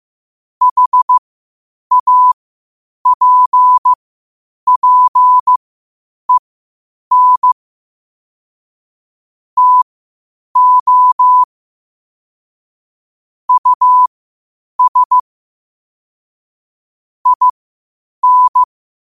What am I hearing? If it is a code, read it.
Morse code, heard as a keyed tone, about 15 wpm, spaced as for 8 wpm, 1000 Hz. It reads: HAPPEN TO US IN
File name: qotdfiles / 8wpm